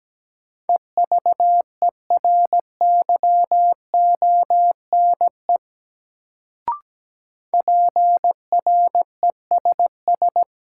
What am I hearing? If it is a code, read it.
Morse code, heard as a keyed tone, about 17 wpm, 700 Hz. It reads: EVERYONE PRESS
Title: Sentences from Top 800 Words - New Word Drill - 17wpm